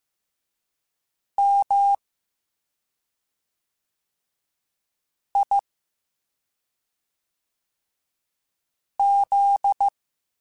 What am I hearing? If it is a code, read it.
Morse code, heard as a keyed tone, about 15 wpm, 800 Hz. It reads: M I Z